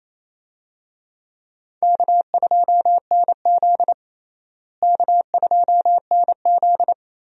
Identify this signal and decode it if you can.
Morse code, heard as a keyed tone, about 28 wpm, 700 Hz. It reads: K2NZ K2NZ
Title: USA Call Signs - 4 Characters - Repeated 3x - 28wpm